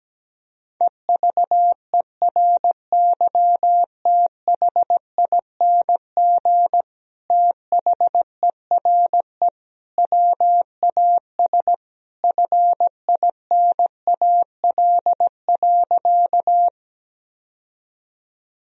Morse code, heard as a keyed tone, about 17 wpm, 700 Hz. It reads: EVERYTHING THERE WAS FINAL.
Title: Sentences from Top 1000 Words - Review Part 2 - 17wpm